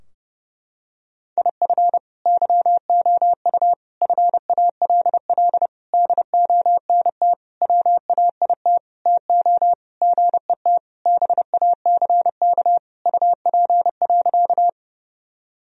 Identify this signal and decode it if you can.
Morse code, heard as a keyed tone, about 30 wpm, 700 Hz. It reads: IF YOU FALL DONT WAIT TO GET BACK UP.
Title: Sentences from Top 1000 Words - Review Part 2 - 30wpm